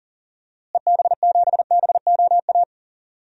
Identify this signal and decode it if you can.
Morse code, heard as a keyed tone, about 40 wpm, 700 Hz. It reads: EB7BOA
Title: DXCC-100 Call Signs - 6 Characters - Repeated 3x - 40wpm